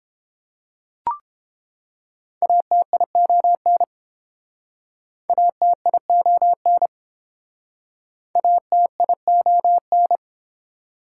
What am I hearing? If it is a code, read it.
Morse code, heard as a keyed tone, about 32 wpm, 700 Hz. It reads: ATION ATION ATION